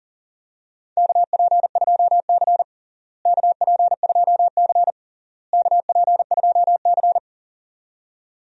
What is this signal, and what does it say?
Morse code, heard as a keyed tone, about 37 wpm, 700 Hz. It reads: KP2C KP2C KP2C
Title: DXCC-100 Call Signs - 4 Characters - Repeated 3x - 40wpm